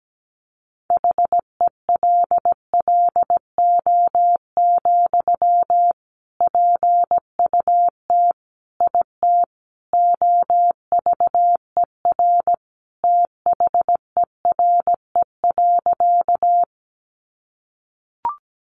Morse code, heard as a keyed tone, about 17 wpm, 700 Hz. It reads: HELLO, PUT IT OVER THERE.
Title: Sentences from Top 1100 Words 17wpm